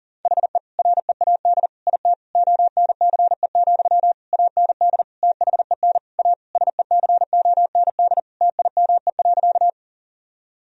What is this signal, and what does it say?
Morse code, heard as a keyed tone, about 40 wpm, 700 Hz. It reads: HE READ IT ONCE, AND THEN A SECOND TIME.
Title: Sentences from Top 1000 Words - Review Part 1 - 40wpm